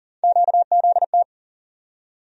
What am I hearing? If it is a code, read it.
Morse code, heard as a keyed tone, about 40 wpm, 700 Hz. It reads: QZT
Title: USA Call Sign Postfixes - 3 Letters 40wpm